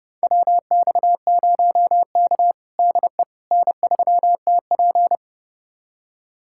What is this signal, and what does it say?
Morse code, heard as a keyed tone, about 31 wpm, 700 Hz. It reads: WX0K DE N3TP
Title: USA Callsign DE Callsign - 4 Characters - 30wpm